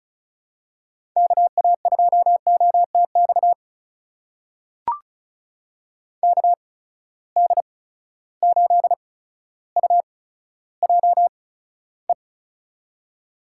Morse code, heard as a keyed tone, about 36 wpm, spaced as for 15 wpm, 700 Hz. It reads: KA2OTX KD8UJE
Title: USA Call Signs - 6 Characters - 8x Character Spacing - 35wpm